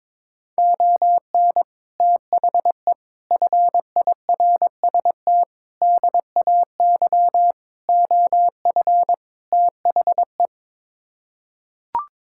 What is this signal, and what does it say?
Morse code, heard as a keyed tone, about 22 wpm, 700 Hz. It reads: ON THE FIRST DAY OF THE
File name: practice / 22wpm / Sets of 6 Words - 22wpm